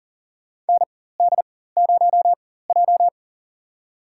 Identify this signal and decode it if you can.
Morse code, heard as a keyed tone, about 40 wpm, spaced as for 19 wpm, 700 Hz. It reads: ND0J
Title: USA Call Signs - 4 Characters - 4x Character Spacing - 40wpm